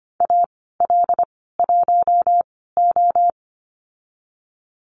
Morse code, heard as a keyed tone, about 25 wpm, spaced as for 16 wpm, 700 Hz. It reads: AL1O
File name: Lesson 15g - Number 5 - Call Signs 25wpm